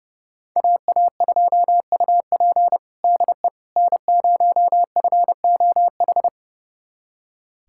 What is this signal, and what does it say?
Morse code, heard as a keyed tone, about 30 wpm, 700 Hz. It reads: AA2UP DE N0FOH
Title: USA Callsign DE Callsign - 5 Characters - 30wpm